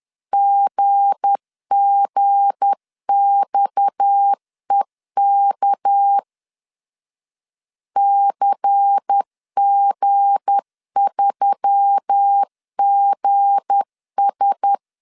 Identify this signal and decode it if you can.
Morse code, heard as a keyed tone, about 10 wpm, 800 Hz. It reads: GGXEK CG3GS
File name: LEZIONI / 7 GF9Z / 7 G 10wpm 2501